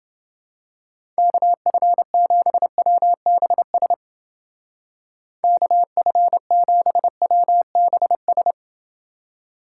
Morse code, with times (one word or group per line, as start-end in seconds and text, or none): 1.2-3.9 KF7WBS
5.4-8.5 KF7WBS